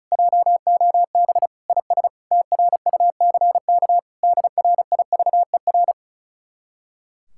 Morse code, heard as a keyed tone, about 44 wpm, 700 Hz. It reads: JOB IS TRUCK DRIVER
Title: Speed-Racing Common QSO Phrases - 35wpm